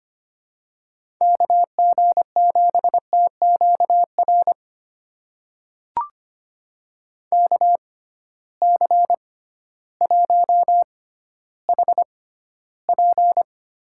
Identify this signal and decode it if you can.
Morse code, heard as a keyed tone, about 25 wpm, spaced as for 13 wpm, 700 Hz. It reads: KG7TQR KC1HP
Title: USA Call Signs - 6 Characters - 6x Character Spacing - 25wpm